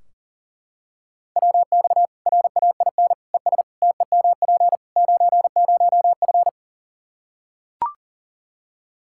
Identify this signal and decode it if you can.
Morse code, heard as a keyed tone, about 40 wpm, 700 Hz. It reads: WX RAIN ES TEMP 90F
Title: Practice QSO - Exchange 2 - Info - 40wpm